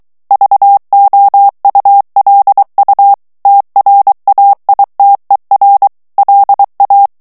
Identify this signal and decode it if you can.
Morse code, heard as a keyed tone, about 23 wpm, 800 Hz. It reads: VOULU TRAITER LA
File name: CW / FICHIERS-MP3 / FRANCAIS TEXTE CLAIR 2 VITESSE 22 MOTS MINUTES